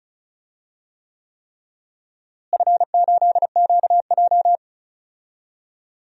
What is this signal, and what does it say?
Morse code, heard as a keyed tone, about 36 wpm, 700 Hz. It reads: F8QJ
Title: DXCC-100 Call Signs - 4 Characters - Repeated 3x - 35wpm